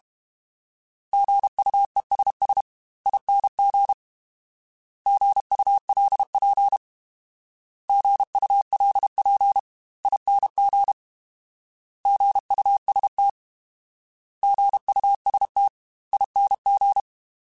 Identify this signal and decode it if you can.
Morse code, heard as a keyed tone, about 32 wpm, spaced as for 28 wpm, 800 Hz. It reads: GUESSING GULP GULPING GUST GUSTING
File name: Alphabetical / Alphabetical mp3s / Gs @30